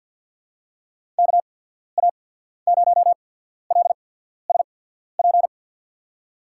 Morse code, heard as a keyed tone, about 53 wpm, spaced as for 14 wpm, 690 Hz. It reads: KA0LSP
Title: USA Call Signs - 6 Characters - 8x Character Spacing - 50wpm